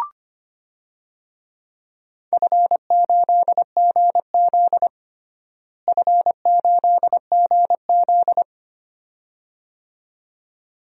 Morse code, heard as a keyed tone, about 25 wpm, 700 Hz. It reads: F8GZ F8GZ
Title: DXCC-100 Call Signs - 4 Characters - Repeated 2x - 25wpm